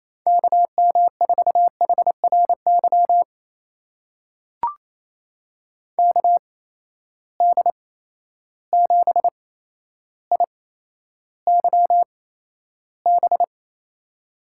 Morse code, heard as a keyed tone, about 28 wpm, spaced as for 12 wpm, 700 Hz. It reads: KM4HRY KD7IYB
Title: USA Call Signs - 6 Characters - 8x Character Spacing - 28wpm